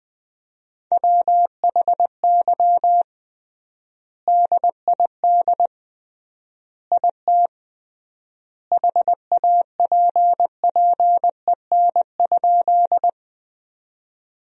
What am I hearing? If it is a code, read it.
Morse code, heard as a keyed tone, about 20 wpm, 700 Hz. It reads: WHY DID IT HAPPEN?
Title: 3x Word-Spacing - Sentences from Top 400 Words - 20wpm